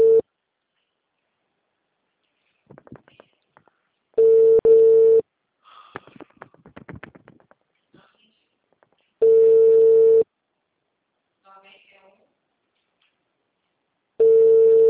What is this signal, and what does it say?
Ringback tone (China, 450 Hz), with speech.